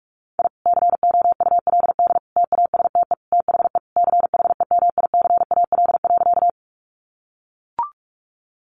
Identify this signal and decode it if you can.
Morse code, heard as a keyed tone, about 44 wpm, 700 Hz. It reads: I COULD TASTE THE CHEMICAL.
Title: Sentences from Top 1200 Words 45wpm